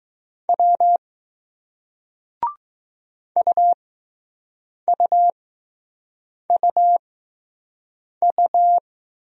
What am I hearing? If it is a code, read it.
Morse code, heard as a keyed tone, about 21 wpm, spaced as for 6 wpm, 700 Hz. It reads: W UUUU